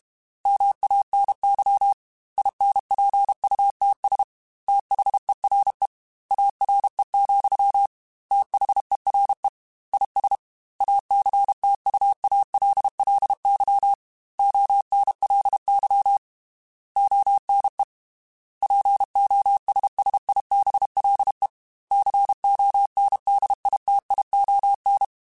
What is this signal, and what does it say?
Morse code, heard as a keyed tone, about 32 wpm, 800 Hz. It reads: MANY INPUTS THERE ARE, THERE IS ACTUALLY ONLY ONE POSSIBLE CONDITION
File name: Technical Subjects / Technical Subjects-mp3s / AND Gate @30